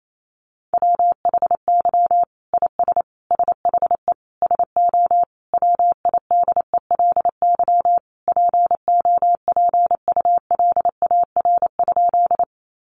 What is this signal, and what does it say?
Morse code, heard as a keyed tone, about 28 wpm, 700 Hz. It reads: WHY IS SHE SO WIDELY POPULAR?
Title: Sentences from Top 1200 Words 28wpm